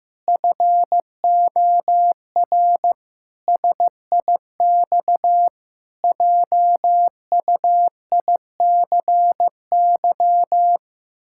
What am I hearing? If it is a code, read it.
Morse code, heard as a keyed tone, about 15 wpm, 700 Hz. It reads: FOR SIX JUICY
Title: Pangram Challenge v2 - 15wpm